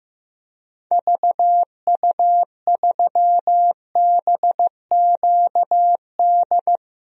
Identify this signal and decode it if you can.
Morse code, heard as a keyed tone, about 15 wpm, 700 Hz. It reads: VU3BQD
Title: DXCC-100 Call Signs - 6 Characters - 15wpm